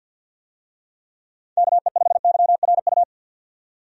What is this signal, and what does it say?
Morse code, heard as a keyed tone, about 47 wpm, 700 Hz. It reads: KE5YRU
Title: USA Call Signs - 6 Characters - 50wpm